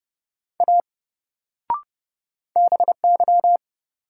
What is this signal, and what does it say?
Morse code, heard as a keyed tone, about 32 wpm, 700 Hz. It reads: A BY
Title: Sentences from Top 100 Words - New Word Drill - 30wpm